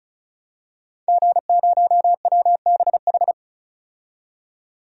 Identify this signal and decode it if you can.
Morse code, heard as a keyed tone, about 36 wpm, 700 Hz. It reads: G0WBH
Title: DXCC-100 Call Signs - 5 Characters - 35wpm